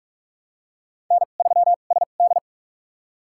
Morse code, heard as a keyed tone, about 46 wpm, spaced as for 32 wpm, 700 Hz. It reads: N3SD
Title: USA Call Signs - 2x Character Spacing - 45wpm